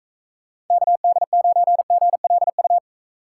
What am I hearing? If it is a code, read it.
Morse code, heard as a keyed tone, about 38 wpm, 700 Hz. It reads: KD9GLU